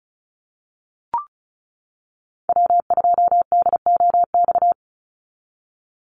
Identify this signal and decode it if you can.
Morse code, heard as a keyed tone, about 34 wpm, 700 Hz. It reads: W2DOX
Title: USA Call Signs 35wpm